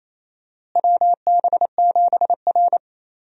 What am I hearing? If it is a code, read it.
Morse code, heard as a keyed tone, about 28 wpm, 700 Hz. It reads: WB7R
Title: USA Call Signs - 4 Characters - Repeated 2x - 28wpm